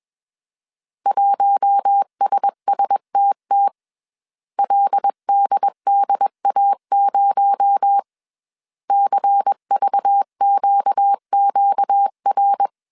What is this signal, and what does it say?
Morse code, heard as a keyed tone, about 21 wpm, 800 Hz. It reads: 1SSTT LDDA0 CVQQR
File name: CW / DET / 1 6 1500C S A 20wpm 2048